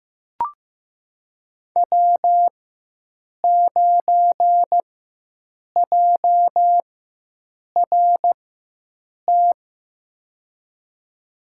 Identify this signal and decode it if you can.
Morse code, heard as a keyed tone, about 15 wpm, spaced as for 7 wpm, 700 Hz. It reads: W9JRT
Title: USA Call Signs - 4x Character Spacing - 15wpm